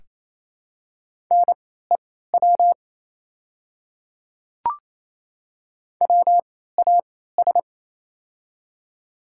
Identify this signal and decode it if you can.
Morse code, heard as a keyed tone, about 28 wpm, spaced as for 16 wpm, 700 Hz. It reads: NEW WAS
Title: 3x Character Spacing - Top 200 Words - 28wpm